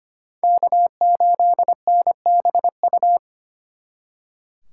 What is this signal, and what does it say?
Morse code, heard as a keyed tone, about 33 wpm, 700 Hz. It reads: K8NBU